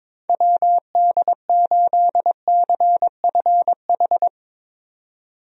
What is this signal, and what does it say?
Morse code, heard as a keyed tone, about 22 wpm, 700 Hz. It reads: WD8CFH
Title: USA Call Signs - 6 Characters - Repeated 2x - 22wpm